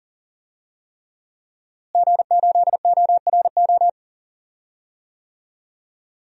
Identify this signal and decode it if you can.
Morse code, heard as a keyed tone, about 40 wpm, 700 Hz. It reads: G8ORO
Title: DXCC-100 Call Signs - 5 Characters - Repeated 3x - 40wpm